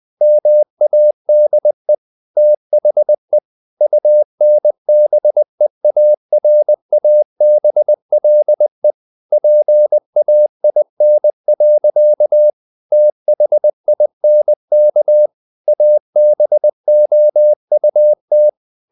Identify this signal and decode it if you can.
Morse code, heard as a keyed tone, about 20 wpm, 600 Hz. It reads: MADE THE UNBEARABLE PAIN. THINK ABOUT
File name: the old man and the sea part 2 20wpm